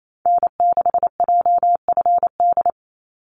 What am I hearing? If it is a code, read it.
Morse code, heard as a keyed tone, about 28 wpm, 700 Hz. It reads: N6JFD
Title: NA SOTA Activator Callsigns 28wpm